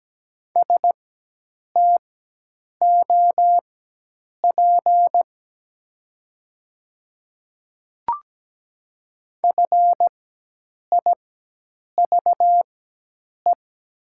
Morse code, heard as a keyed tone, about 17 wpm, spaced as for 8 wpm, 700 Hz. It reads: STOP FIVE
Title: 4x Character Spacing - Top 500 Words - 17wpm